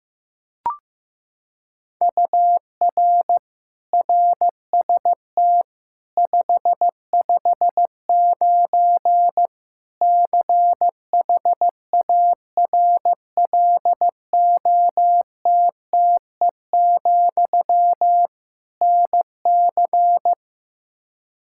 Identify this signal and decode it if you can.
Morse code, heard as a keyed tone, about 15 wpm, 700 Hz. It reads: UR RST 559 CHARLOTTE, NC
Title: Practice QSO - Exchange 1 - Beginning - 15wpm